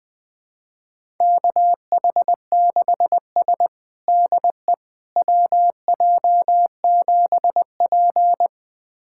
Morse code, heard as a keyed tone, about 20 wpm, 700 Hz. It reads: KH6S DE WJ7P
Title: USA Callsign DE Callsign - 4 Characters - 20wpm